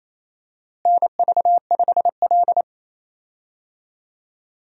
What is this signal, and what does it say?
Morse code, heard as a keyed tone, about 28 wpm, 700 Hz. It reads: NV5L
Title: USA Call Signs - 4 Characters - Repeated 3x - 28wpm